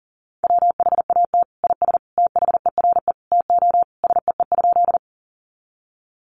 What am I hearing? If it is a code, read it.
Morse code, heard as a keyed tone, about 40 wpm, 700 Hz. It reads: WHAT IS THERE TO SEE?
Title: Sentences from Top 100 Words 40wpm